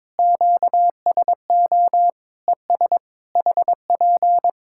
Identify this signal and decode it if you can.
Morse code, heard as a keyed tone, about 22 wpm, 700 Hz. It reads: QSO ES HP
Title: Practice QSO - Exchange 4 - Ending - 22wpm